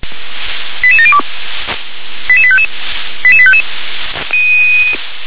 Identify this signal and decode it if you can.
Five-tone selective call (ZVEI), heard as quick runs of five tones, tones of about 70 ms. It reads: 88702 88855 88855